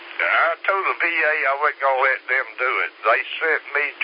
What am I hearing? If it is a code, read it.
Emergency Alert System (SAME) tones, heard as harsh buzzing bursts, with speech. It is heard from 1.0 to 3.7 s.